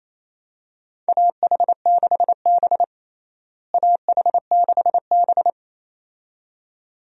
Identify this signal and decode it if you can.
Morse code, heard as a keyed tone, about 28 wpm, 700 Hz. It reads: AH6B AH6B
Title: USA Call Signs - 4 Characters - Repeated 2x - 28wpm